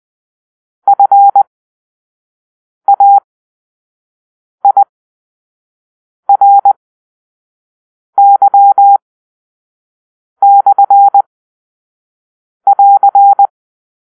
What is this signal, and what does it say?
Morse code, heard as a keyed tone, about 20 wpm, spaced as for 5 wpm, 800 Hz. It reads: FAIRY/<AR>
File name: arrl / Grp-4 Drill-1